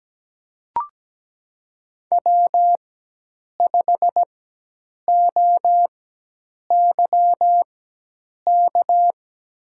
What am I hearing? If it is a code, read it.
Morse code, heard as a keyed tone, about 17 wpm, spaced as for 8 wpm, 700 Hz. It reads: W5OYK
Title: USA Call Signs - 5 Characters - 4x Character Spacing - 17wpm